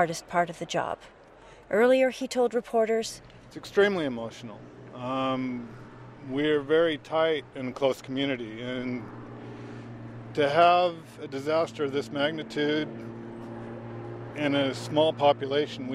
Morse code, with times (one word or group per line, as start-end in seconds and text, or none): none